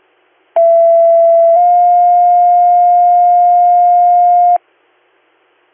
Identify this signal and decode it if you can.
Two-tone page, heard as a one-second tone, then a three-second tone, about 670 then 710 Hz. It is heard from 0.6 to 4.6 s.